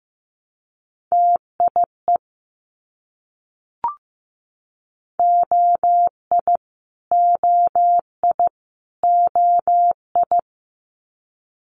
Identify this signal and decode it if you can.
Morse code, heard as a keyed tone, about 15 wpm, 700 Hz. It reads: TIE OI OI OI
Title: Multi-letter Phonemes Repeated 3x 15wpm